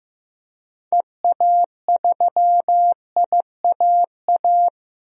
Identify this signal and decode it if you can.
Morse code, heard as a keyed tone, about 15 wpm, 700 Hz. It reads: EA3IAA
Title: DXCC-100 Call Signs - 6 Characters - 15wpm